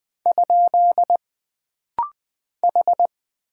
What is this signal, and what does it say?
Morse code, heard as a keyed tone, about 20 wpm, 700 Hz. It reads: ? H
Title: Single Character (Letter:Number:Punctuation:Prosign) - Rapid-Fire - 20wpm